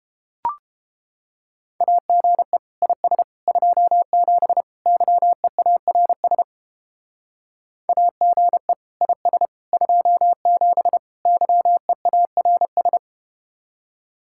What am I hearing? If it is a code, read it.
Morse code, heard as a keyed tone, about 32 wpm, 700 Hz. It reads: AGE IS 27 YEARS AGE IS 27 YEARS